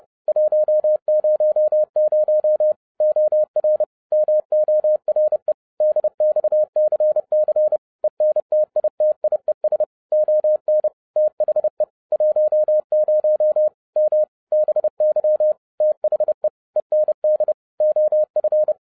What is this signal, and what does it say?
Morse code, heard as a keyed tone, about 30 wpm, 610 Hz. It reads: 100 OR MORE DXCC ENTITIES ON THE 10 M BY THE END OF
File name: 2012-01-mp3 / n12-I DIDNT MISS THE SOLAR CYCLE 24-30wpm